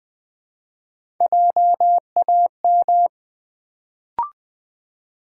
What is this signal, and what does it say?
Morse code, heard as a keyed tone, about 20 wpm, 700 Hz. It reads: JAM
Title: Build A Word - 1 Letter At A Time - 3-Letter Words - 20wpm